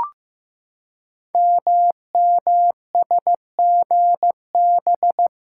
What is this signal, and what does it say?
Morse code, heard as a keyed tone, about 15 wpm, 700 Hz. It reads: MMSGB
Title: Code Groups - 5 Letters - 15wpm